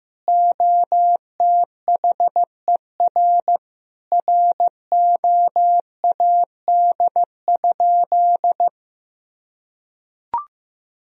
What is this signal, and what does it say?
Morse code, heard as a keyed tone, about 15 wpm, 700 Hz. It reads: OTHER ROAD?
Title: Sentences from Top 1000 Words - Review Part 2 - 15wpm